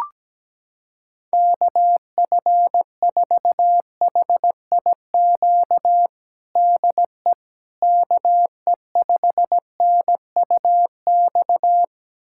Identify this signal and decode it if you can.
Morse code, heard as a keyed tone, about 17 wpm, 700 Hz. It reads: KF4HIQ DE KE5NUX
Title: USA Callsign DE Callsign 17wpm